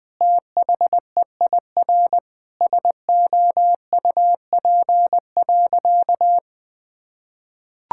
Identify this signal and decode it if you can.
Morse code, heard as a keyed tone, about 21 wpm, 700 Hz. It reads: THEIR SOUP.
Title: Sentences from Top 1000 Words - Review Part 6 - 20wpm